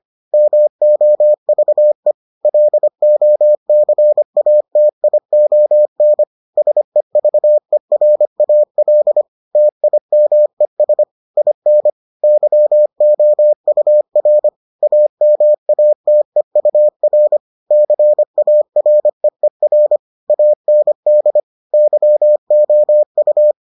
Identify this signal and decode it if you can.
Morse code, heard as a keyed tone, about 25 wpm, 600 Hz. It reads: MOVE LOCATION SEVERAL TIMES IN YOUR AMATEUR CAREER AND YOU